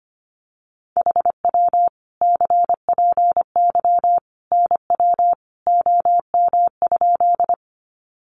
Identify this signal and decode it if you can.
Morse code, heard as a keyed tone, about 25 wpm, 700 Hz. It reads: HW CPY NW OM?
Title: Common QSO Phrases 25wpm